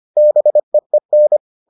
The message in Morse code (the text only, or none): BEEN